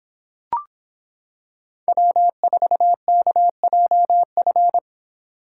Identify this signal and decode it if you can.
Morse code, heard as a keyed tone, about 26 wpm, 700 Hz. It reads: W4KJF